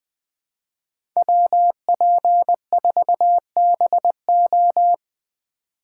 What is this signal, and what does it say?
Morse code, heard as a keyed tone, about 23 wpm, 700 Hz. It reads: WP4BO
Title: Speed Racing - USA Call Signs 20wpm